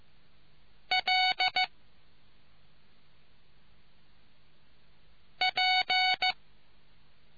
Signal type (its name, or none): Morse code